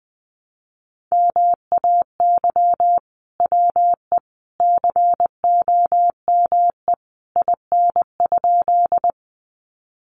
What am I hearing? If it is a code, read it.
Morse code, heard as a keyed tone, about 20 wpm, 700 Hz. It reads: MAY WE COME IN?
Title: Sentences from Top 100 Words 20wpm